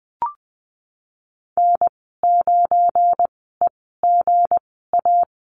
Morse code, heard as a keyed tone, about 20 wpm, spaced as for 15 wpm, 700 Hz. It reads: N9EGA
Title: 219 Lesson 35g - Number 0 - Call Signs 20wpm